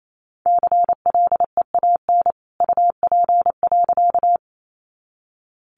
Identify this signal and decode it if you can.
Morse code, heard as a keyed tone, about 28 wpm, 700 Hz. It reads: CLEAN UP.
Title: A Sound Of Thunder 28wpm